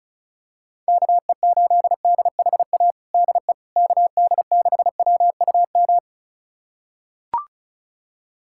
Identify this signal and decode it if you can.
Morse code, heard as a keyed tone, about 36 wpm, 690 Hz. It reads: KE8DHA DE KD6WUM
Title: USA Callsign DE Callsign - 6 Characters - 35wpm